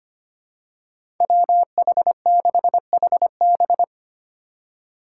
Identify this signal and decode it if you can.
Morse code, heard as a keyed tone, about 25 wpm, 700 Hz. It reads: WH6HB
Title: USA Call Signs - 2x Character Spacing - 25wpm